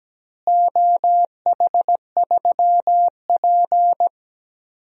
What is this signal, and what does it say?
Morse code, heard as a keyed tone, about 17 wpm, 700 Hz. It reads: OH3P
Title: DXCC-100 Call Signs - 4 Characters - Repeated 2x - 17wpm